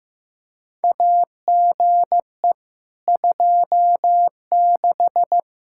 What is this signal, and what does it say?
Morse code, heard as a keyed tone, about 15 wpm, 700 Hz. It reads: AGE 26
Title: Practice QSO - Exchange 3 - More Info - 15wpm